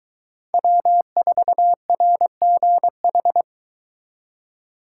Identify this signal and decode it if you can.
Morse code, heard as a keyed tone, about 23 wpm, 700 Hz. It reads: W4RGH